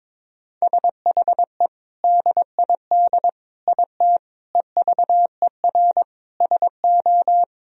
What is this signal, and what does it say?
Morse code, heard as a keyed tone, about 22 wpm, 700 Hz. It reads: SHE DID IT EVER SO
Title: Sentences from Top 400 Words and Abbreviations 22wpm